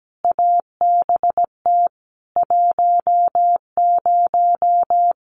Morse code, heard as a keyed tone, about 17 wpm, 700 Hz. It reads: ABT 10
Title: Common QSO Phrases 17wpm